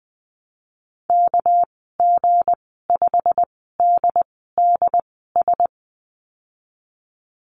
Morse code, heard as a keyed tone, about 20 wpm, spaced as for 14 wpm, 700 Hz. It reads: KG5DDS
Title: 238 Lesson 38g - Letter q - Call Signs 20wpm